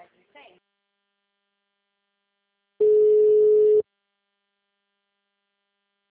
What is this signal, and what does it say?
Ringback tone (Europe, 425 Hz), with speech.
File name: out- -2007- - - .6